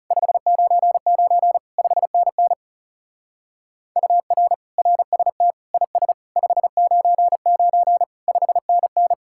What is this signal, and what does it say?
Morse code, heard as a keyed tone, about 37 wpm, 700 Hz. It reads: 599 5NN UR RST IS 599 5NN
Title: Speed-Racing Common QSO Phrases - 30wpm